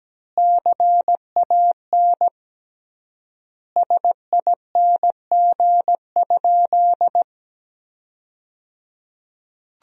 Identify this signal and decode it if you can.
Morse code, heard as a keyed tone, about 17 wpm, 700 Hz. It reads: CAN SING?
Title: 3x Word-Spacing - Sentences from Top 400 Words - 17wpm